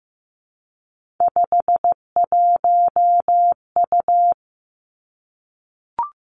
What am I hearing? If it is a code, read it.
Morse code, heard as a keyed tone, about 15 wpm, 700 Hz. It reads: 51U
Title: Three Challenging Letter-Number Combinations 15wpm